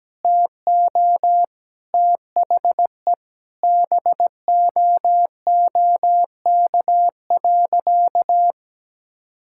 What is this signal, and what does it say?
Morse code, heard as a keyed tone, about 17 wpm, 700 Hz. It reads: TO THE BOOK.
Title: Sentences from Top 1000 Words - Review Part 6 - 17wpm